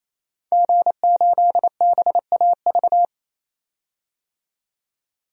Morse code, heard as a keyed tone, about 28 wpm, 700 Hz. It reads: G8BAV
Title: DXCC-100 Call Signs - 5 Characters - 28wpm